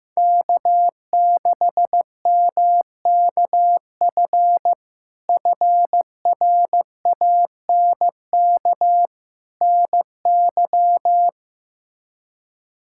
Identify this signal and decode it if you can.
Morse code, heard as a keyed tone, about 15 wpm, 700 Hz. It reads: K6MKF FRANK NY
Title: K1USN SST QSO Practice - 15wpm